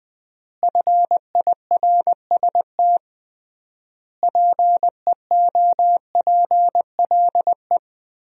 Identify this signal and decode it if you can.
Morse code, heard as a keyed tone, about 20 wpm, 700 Hz. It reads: FIRST PEOPLE
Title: 3x Word-Spacing - Sentences from Top 300 Words - 20wpm